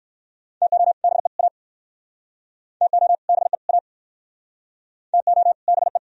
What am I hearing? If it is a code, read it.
Morse code, heard as a keyed tone, about 62 wpm, 700 Hz. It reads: TO BE A TO BE A TO BE